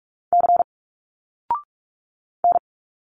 Morse code, heard as a keyed tone, about 46 wpm, 700 Hz. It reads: C N
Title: Single Letters 45wpm